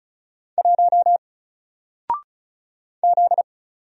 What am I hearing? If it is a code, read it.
Morse code, heard as a keyed tone, about 36 wpm, 700 Hz. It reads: 1 Z
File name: Single-Letter-Number-35wpm